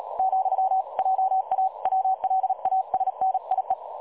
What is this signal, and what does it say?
Morse code, heard as a keyed tone, about 37 wpm, 760 Hz. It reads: ,JA WLASNIE
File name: dlazaawansowanych / nagraniazpasma / websdr recording 3565.0kHz